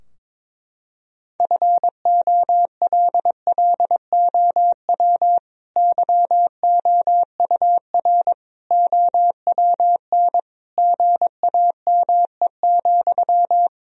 Morse code, heard as a keyed tone, about 22 wpm, 700 Hz. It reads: FOLLOW YOUR OWN GAME,
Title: Sentences from Top 1000 Words - Review Part 3 - 22wpm